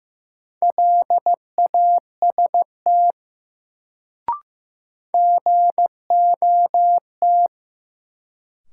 Morse code, heard as a keyed tone, about 16 wpm, 700 Hz. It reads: LAST GOT
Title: Sentences from Top 300 Words - New Word Drill - 15wpm